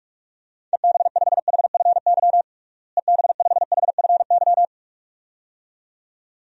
Morse code, heard as a keyed tone, about 41 wpm, 700 Hz. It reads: EB5HFY EB5HFY
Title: DXCC-100 Call Signs - 6 Characters - Repeated 2x - 45wpm